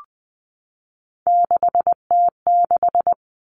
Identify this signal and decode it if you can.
Morse code, heard as a keyed tone, about 20 wpm, 700 Hz. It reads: 6T6
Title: All Cut Numbers - 3-Digit 20wpm